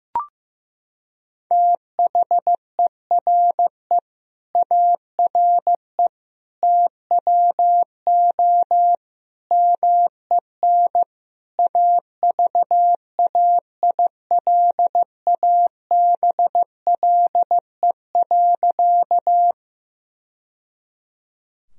Morse code, heard as a keyed tone, about 15 wpm, 700 Hz. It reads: THERE ARE TWO MEN AVAILABLE.
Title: Sentences from Top 1000 Words - Review Part 3 - 15wpm